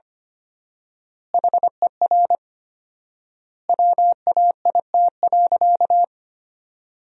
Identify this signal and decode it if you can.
Morse code, heard as a keyed tone, about 25 wpm, 700 Hz. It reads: HER WAIT.
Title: 4x Word-Spacing - Sentences from Top 500 Words - 25wpm